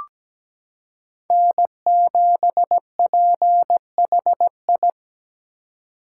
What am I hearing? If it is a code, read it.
Morse code, heard as a keyed tone, about 17 wpm, 700 Hz. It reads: N7PHI
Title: K1USN SST QSO Practice - 17wpm